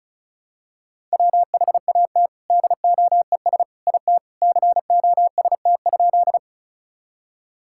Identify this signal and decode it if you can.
Morse code, heard as a keyed tone, about 36 wpm, 700 Hz. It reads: WHAT DOES IT COST?
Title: Sentences from Top 1000 Words - Review Part 5 - 35wpm